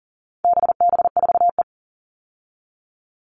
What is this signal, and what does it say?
Morse code, heard as a keyed tone, about 40 wpm, 700 Hz. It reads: BB4E
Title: All Cut Numbers - 4-Digit 40wpm